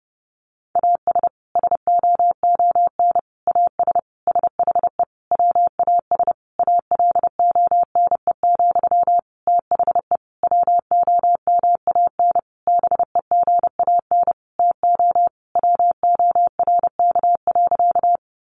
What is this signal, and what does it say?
Morse code, heard as a keyed tone, about 30 wpm, 700 Hz. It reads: AS SOON AS SHE WAS ALONE, THE WOMAN BEGAN TO WORK.
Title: Sentences from Top 1100 Words 30wpm